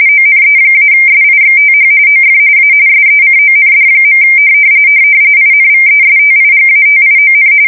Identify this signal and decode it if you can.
RTTY, about 45.45 baud. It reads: ONE-SIXTH OF A SECOND. INVESTIGATION
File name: fsk170 / printing-telegraph